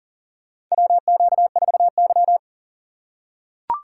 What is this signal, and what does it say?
Morse code, heard as a keyed tone, about 40 wpm, 700 Hz. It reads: WQ4Y
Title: USA Call Signs - 4 Characters - 3x Character Spacing - 40wpm